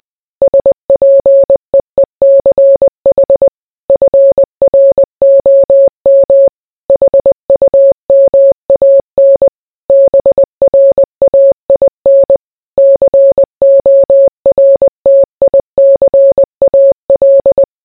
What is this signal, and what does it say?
Morse code, heard as a keyed tone, about 20 wpm, 550 Hz. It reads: SPEECH FROM HUMAN BRAIN CORTICAL